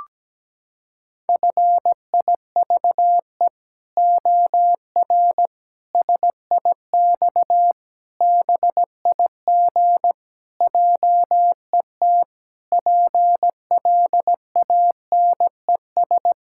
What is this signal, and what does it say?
Morse code, heard as a keyed tone, about 17 wpm, 700 Hz. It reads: FIVE OR SIX BIG JET PLANES
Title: Pangram Challenge v2 - 17wpm